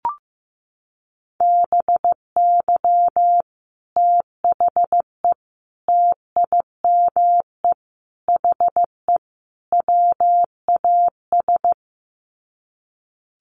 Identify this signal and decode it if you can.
Morse code, heard as a keyed tone, about 15 wpm, 700 Hz. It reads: BY THE TIME HE WAS
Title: Sets of 5 Words Encore 15wpm